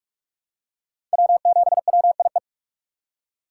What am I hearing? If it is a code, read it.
Morse code, heard as a keyed tone, about 41 wpm, 700 Hz. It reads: W7WIE
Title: USA Call Signs - 2x Character Spacing - 45wpm